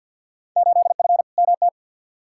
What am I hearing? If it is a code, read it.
Morse code, heard as a keyed tone, about 51 wpm, 700 Hz. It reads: 8F MT
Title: Field Day Exchanges Encore - 50wpm